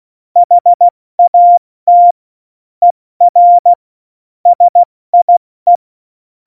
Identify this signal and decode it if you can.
Morse code, heard as a keyed tone, about 16 wpm, 700 Hz. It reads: HAT ER SIE
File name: Morsekurs / Witze / Witz 24. November 2020, 70 F80